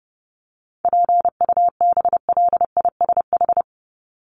Morse code, heard as a keyed tone, about 30 wpm, 700 Hz. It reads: PUBLISH
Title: ED Suffix - Top 100 Words with Base Words 30wpm